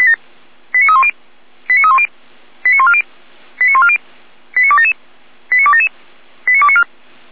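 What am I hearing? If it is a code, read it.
Five-tone selective call (ZVEI), heard as quick runs of five tones, tones of about 70 ms. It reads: 89310 89310 89140 89140 89277 89277 89294
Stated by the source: ZVEI five-tone selective call received by SDR.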